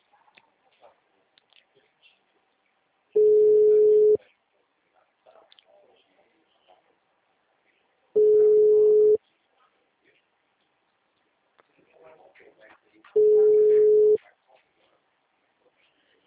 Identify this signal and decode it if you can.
Ringback tone (Europe, 425 Hz), with speech.